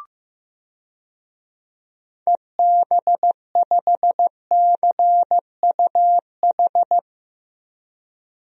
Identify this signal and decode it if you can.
Morse code, heard as a keyed tone, about 15 wpm, 700 Hz. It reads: EB5CUH
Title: DXCC-100 Call Signs - 6 Characters - Repeated 3x - 15wpm